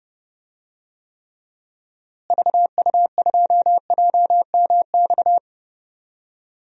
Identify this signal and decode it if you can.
Morse code, heard as a keyed tone, about 30 wpm, 700 Hz. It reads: VU2JMX
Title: DXCC-100 Call Signs - 6 Characters - Repeated 3x - 30wpm